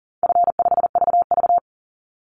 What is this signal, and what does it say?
Morse code, heard as a keyed tone, about 42 wpm, 700 Hz. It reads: F5VV
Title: Easy Call Signs - 40wpm